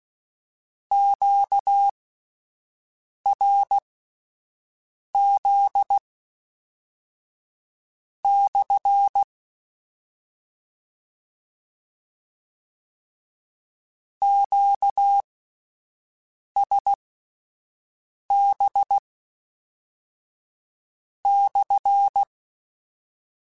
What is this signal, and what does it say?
Morse code, heard as a keyed tone, about 16 wpm, spaced as for 5 wpm, 800 Hz. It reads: QRZ/ QSB/
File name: Q Calls / Q Calls-mp3s / Q Calls SPSL1 @10